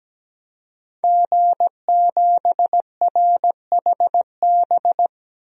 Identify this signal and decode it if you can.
Morse code, heard as a keyed tone, about 17 wpm, 700 Hz. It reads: G7RHB
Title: DXCC-100 Call Signs - 5 Characters - 17wpm